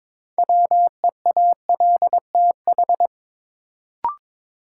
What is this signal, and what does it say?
Morse code, heard as a keyed tone, about 22 wpm, 700 Hz. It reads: WEALTH
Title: Sentences from Top 1200 Words - New Word Drill Alt - 22wpm